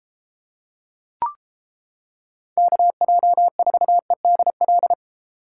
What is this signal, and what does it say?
Morse code, heard as a keyed tone, about 31 wpm, 700 Hz. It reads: KJ4EDL